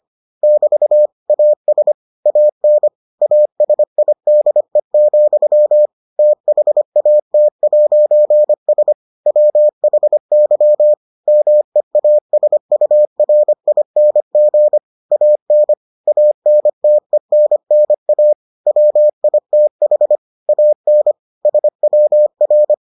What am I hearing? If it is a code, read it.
Morse code, heard as a keyed tone, about 25 wpm, 600 Hz. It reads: <BT> AS AN ASIDE, THAT'S WHY MEASURING AN ANTENNA WITH AN SWR